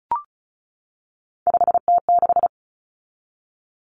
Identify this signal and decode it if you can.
Morse code, heard as a keyed tone, about 34 wpm, 700 Hz. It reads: 5T6
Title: Common Cut Numbers - 3-Digit 35wpm